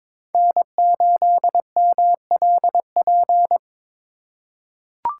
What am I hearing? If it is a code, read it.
Morse code, heard as a keyed tone, about 21 wpm, 700 Hz. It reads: N8MLP
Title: USA Call Signs - 5 Characters - 2x Character Spacing - 22wpm